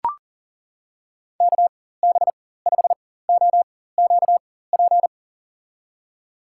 Morse code, heard as a keyed tone, about 40 wpm, spaced as for 19 wpm, 700 Hz. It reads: KB5OQP
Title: USA Call Signs - 6 Characters - 4x Character Spacing - 40wpm